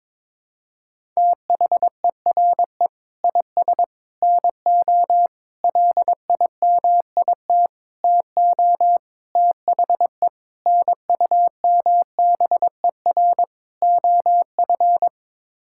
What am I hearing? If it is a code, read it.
Morse code, heard as a keyed tone, about 22 wpm, 700 Hz. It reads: THERE IS NO LIMIT TO THE NUMBER OF
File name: practice / 22wpm / Sets of 8 Words - 22wpm